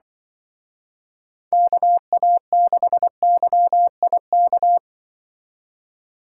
Morse code, heard as a keyed tone, about 24 wpm, 700 Hz. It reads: KA6YIK
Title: Speed Racing - USA 2x3 Call Signs 20wpm